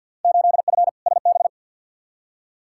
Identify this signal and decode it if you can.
Morse code, heard as a keyed tone, about 50 wpm, 700 Hz. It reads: 8F SB
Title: Field Day Exchanges Encore - 50wpm